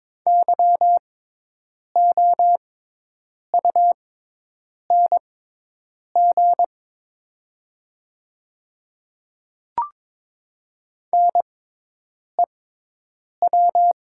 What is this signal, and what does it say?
Morse code, heard as a keyed tone, about 22 wpm, spaced as for 8 wpm, 690 Hz. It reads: YOUNG NEW
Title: 6x Character Spacing - Top 500 Words - 22wpm